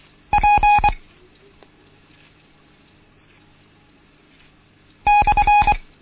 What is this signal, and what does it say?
Morse code, heard as a keyed tone, about 24 wpm, 830 Hz. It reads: P /
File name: Morsen / Kurs A 16